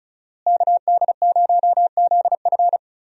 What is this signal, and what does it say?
Morse code, heard as a keyed tone, about 34 wpm, 700 Hz. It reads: KD0ZF
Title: USA Call Signs - 5 Characters - Repeated 3x - 35wpm